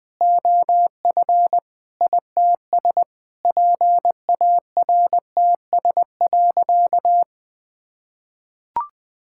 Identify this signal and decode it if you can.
Morse code, heard as a keyed tone, about 20 wpm, 700 Hz. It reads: OF ITS PARTS.
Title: Sentences with 7-Word Phrases - 20wpm